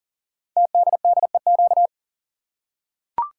Morse code, heard as a keyed tone, about 38 wpm, 690 Hz. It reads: TDDEQ
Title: Code Groups - 5 Letters - 40wpm